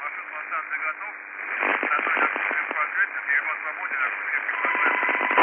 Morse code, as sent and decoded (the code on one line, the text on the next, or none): none